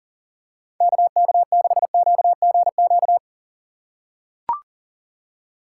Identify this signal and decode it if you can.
Morse code, heard as a keyed tone, about 40 wpm, 700 Hz. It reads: KK6QGQ
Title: USA Call Signs - 6 Characters - 2x Character Spacing - 40wpm